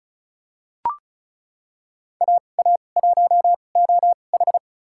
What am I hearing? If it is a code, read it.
Morse code, heard as a keyed tone, about 34 wpm, spaced as for 25 wpm, 700 Hz. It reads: AA1OH
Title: USA Call Signs - 5 Characters - 2x Character Spacing - 35wpm